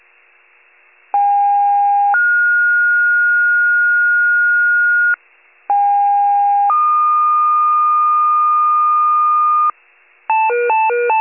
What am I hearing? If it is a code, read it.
Two-tone page, heard as a one-second tone, then a three-second tone, about 810 then 1430 Hz, about 810 then 1220 Hz, with noise. It is heard from 1.1 to 9.7 s.